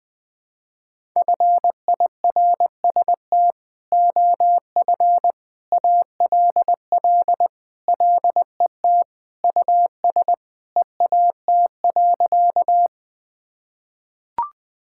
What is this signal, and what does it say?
Morse code, heard as a keyed tone, about 21 wpm, 700 Hz. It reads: FIRST OF ALL LET US EAT.
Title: Sentences with 3-Word Phrases - 20wpm